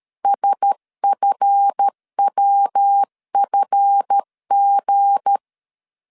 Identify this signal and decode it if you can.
Morse code, heard as a keyed tone, about 13 wpm, 800 Hz. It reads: SFWFG
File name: LEZIONI / 7 GF9Z / 7 GF 12wpm 2122